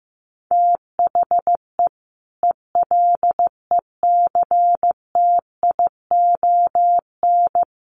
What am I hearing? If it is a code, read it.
Morse code, heard as a keyed tone, about 15 wpm, 700 Hz. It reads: THE ELECTION
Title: Sentences from Top 1500 Words 15wpm